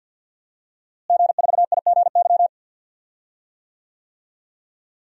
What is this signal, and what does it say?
Morse code, heard as a keyed tone, about 48 wpm, 700 Hz. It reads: G4IGY
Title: DXCC-100 Call Signs - 5 Characters - Repeated 2x - 50wpm